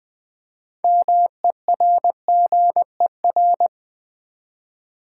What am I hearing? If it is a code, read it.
Morse code, heard as a keyed tone, about 20 wpm, 700 Hz. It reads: MERGER
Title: Build A Word - 2 Letters At A Time - 6-Letter Words - 20wpm